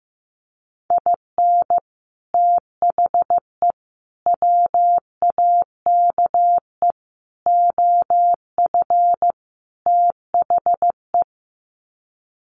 Morse code, heard as a keyed tone, about 15 wpm, 700 Hz. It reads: IN THE WAKE OF THE
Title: Sets of 5 Words 15wpm